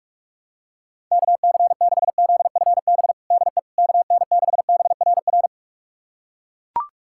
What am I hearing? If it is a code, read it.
Morse code, heard as a keyed tone, about 41 wpm, 700 Hz. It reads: KC6ZFB DE KN6BRR
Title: USA Callsign DE Callsign 45wpm